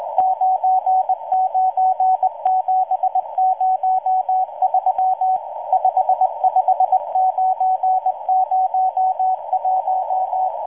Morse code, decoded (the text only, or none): 99703 55901